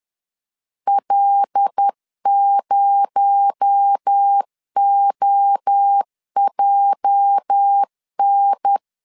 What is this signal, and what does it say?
Morse code, heard as a keyed tone, about 10 wpm, 800 Hz. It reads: L0OJN
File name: CW / DET / 1 9 1500C S B 10wpm 4439